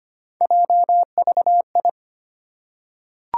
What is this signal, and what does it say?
Morse code, heard as a keyed tone, about 25 wpm, 700 Hz. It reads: JVI
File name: Three Letter-Number Combinations - 25wpm